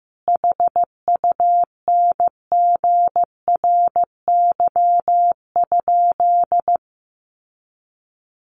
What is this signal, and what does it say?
Morse code, heard as a keyed tone, about 14 wpm, 700 Hz. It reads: HUNGRY?
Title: Sentences from Top 1400 Words 15wpm